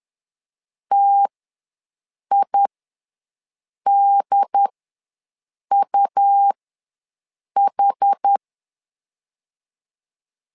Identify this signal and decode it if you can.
Morse code, heard as a keyed tone, about 10 wpm, spaced as for 6 wpm, 800 Hz. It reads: TIDUH
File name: LEZIONI / 2 UHDN / 2 UHD ws 10wpm 2058